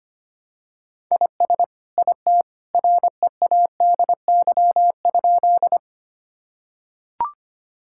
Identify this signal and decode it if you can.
Morse code, heard as a keyed tone, about 25 wpm, 690 Hz. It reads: IS IT READY?
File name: practice / 25wpm / Sentences with 2-Word Phrases - 25wpm